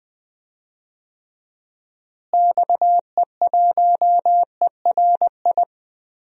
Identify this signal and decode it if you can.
Morse code, heard as a keyed tone, about 20 wpm, 700 Hz. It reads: XE1ERI
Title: DXCC-100 Call Signs - 6 Characters - Repeated 3x - 20wpm